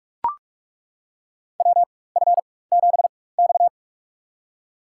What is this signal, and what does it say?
Morse code, heard as a keyed tone, about 42 wpm, spaced as for 21 wpm, 700 Hz. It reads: WF7X
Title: USA Call Signs - 4 Characters - 4x Character Spacing - 45wpm